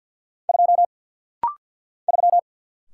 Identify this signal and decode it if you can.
Morse code, heard as a keyed tone, about 50 wpm, 690 Hz. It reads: 2 3
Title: Single Letter-Number - Rapid-Fire 50wpm